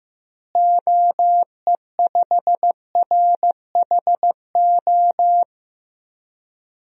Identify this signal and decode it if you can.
Morse code, heard as a keyed tone, about 15 wpm, 700 Hz. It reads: OE5RHO
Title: DXCC-100 Call Signs - 6 Characters - Repeated 3x - 15wpm